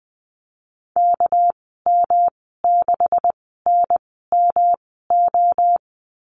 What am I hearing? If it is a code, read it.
Morse code, heard as a keyed tone, about 20 wpm, spaced as for 14 wpm, 700 Hz. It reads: KM6NMO
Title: 200 Lesson 32g - Letter k - Call Signs 20wpm